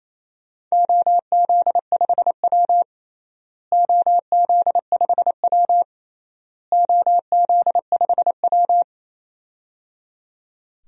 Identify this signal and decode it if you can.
Morse code, heard as a keyed tone, about 28 wpm, 700 Hz. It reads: OZ5W OZ5W OZ5W
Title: DXCC-100 Call Signs - 4 Characters - Repeated 3x - 28wpm